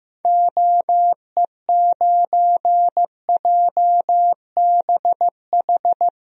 Morse code, heard as a keyed tone, about 16 wpm, 700 Hz. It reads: OE9JBH
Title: DXCC-100 Call Signs - 6 Characters - 15wpm